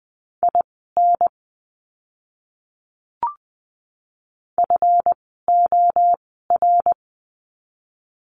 Morse code, heard as a keyed tone, about 20 wpm, spaced as for 15 wpm, 700 Hz. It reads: IN FOR
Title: 244 Lesson 39f - Letter z - Words 20wpm